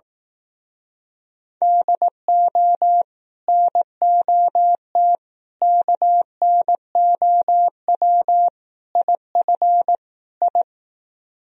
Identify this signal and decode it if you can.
Morse code, heard as a keyed tone, about 18 wpm, 700 Hz. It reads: DO NOT KNOW IF I
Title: Speed-Racing - Sets of 5 Words 15wpm